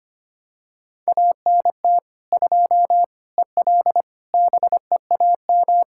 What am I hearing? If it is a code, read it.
Morse code, heard as a keyed tone, about 25 wpm, 700 Hz. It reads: ANT 2 EL BEAM
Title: Practice QSO - Exchange 2 - Info - 25wpm